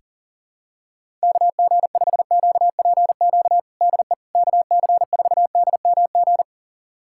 Morse code, heard as a keyed tone, about 40 wpm, 700 Hz. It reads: KG5QPQ DE KC4DMG
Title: USA Callsign DE Callsign - 6 Characters - 40wpm